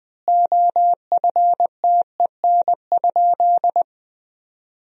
Morse code, heard as a keyed tone, about 20 wpm, 700 Hz. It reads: OFTEN?
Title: 3x Word-Spacing - Sentences from Top 400 Words - 20wpm